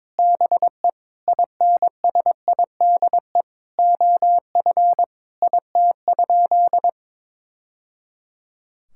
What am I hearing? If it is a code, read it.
Morse code, heard as a keyed tone, about 22 wpm, 700 Hz. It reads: BE INSIDE OF IT?
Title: Sentences from Top 1000 Words - Review Part 6 - 22wpm